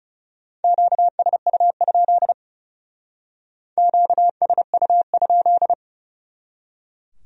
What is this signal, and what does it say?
Morse code, heard as a keyed tone, about 34 wpm, 700 Hz. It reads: QSU? QSU?
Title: Speed-Racing Q-Codes - 30wpm